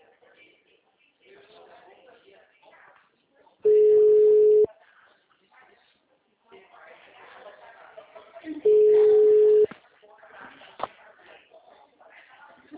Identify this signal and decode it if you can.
Ringback tone (Europe, 425 Hz), with speech.